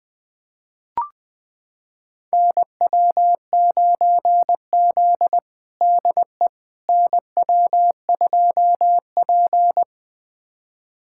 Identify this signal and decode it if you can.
Morse code, heard as a keyed tone, about 20 wpm, 700 Hz. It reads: NW9Z DE NW2P
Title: USA Callsign DE Callsign - 4 Characters - 20wpm